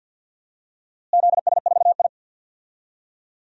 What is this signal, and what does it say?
Morse code, heard as a keyed tone, about 47 wpm, 700 Hz. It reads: ZS4I
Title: DXCC-100 Call Signs - 4 Characters - Repeated 2x - 50wpm